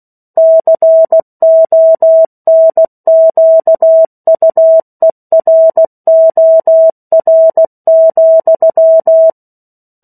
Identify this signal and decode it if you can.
Morse code, heard as a keyed tone, about 16 wpm, 650 Hz. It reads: CONQUEROR,
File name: 03-Chapter-03 16 WPM